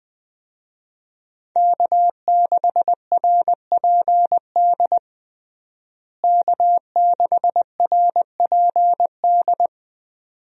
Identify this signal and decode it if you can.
Morse code, heard as a keyed tone, about 20 wpm, 700 Hz. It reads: K6RPD K6RPD
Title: USA Call Signs - 5 Characters - Repeated 2x - 20wpm